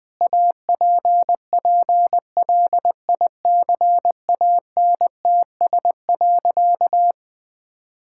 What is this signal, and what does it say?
Morse code, heard as a keyed tone, about 20 wpm, 700 Hz. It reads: APPLICANTS.
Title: Sentences with 6-Word Phrases - 20wpm